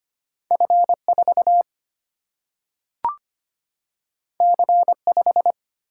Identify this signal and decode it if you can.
Morse code, heard as a keyed tone, about 25 wpm, 700 Hz. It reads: F 4 C 5
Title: Single-Letter Single-Number 25wpm